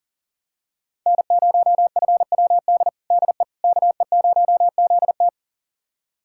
Morse code, heard as a keyed tone, about 40 wpm, 700 Hz. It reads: N0FWD DE KE0ZT
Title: USA Callsign DE Callsign - 5 Characters - 40wpm